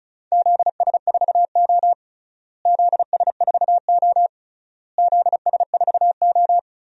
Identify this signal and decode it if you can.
Morse code, heard as a keyed tone, about 36 wpm, 700 Hz. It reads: ZS4O ZS4O ZS4O
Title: DXCC-100 Call Signs - 4 Characters - Repeated 3x - 35wpm